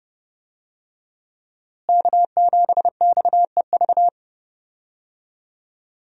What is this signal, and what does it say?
Morse code, heard as a keyed tone, about 31 wpm, 700 Hz. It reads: K7XEV
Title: USA Call Signs - 5 Characters - Repeated 2x - 30wpm